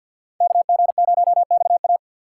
Morse code, heard as a keyed tone, about 51 wpm, 700 Hz. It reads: KG0XA
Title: USA Call Signs - 5 Characters - 2x Character Spacing - 50wpm